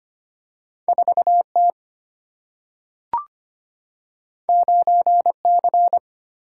Morse code, heard as a keyed tone, about 25 wpm, 700 Hz. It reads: 4T 9C
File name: Two Letter-Number Combinations - 25wpm